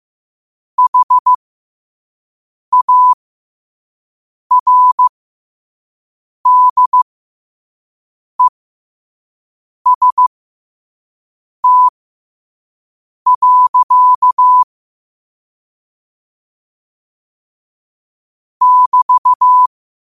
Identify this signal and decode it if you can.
Morse code, heard as a keyed tone, about 15 wpm, spaced as for 5 wpm, 1000 Hz. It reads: HARDEST. <BT>
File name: qotdfiles / 5wpm